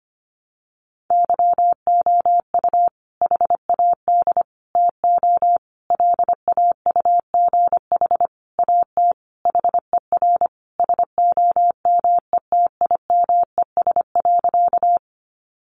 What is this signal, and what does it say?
Morse code, heard as a keyed tone, about 25 wpm, 700 Hz. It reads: YOU HAD TO LAUGH AT HER SOMETIMES.
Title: Sentences from Top 600 Words 25wpm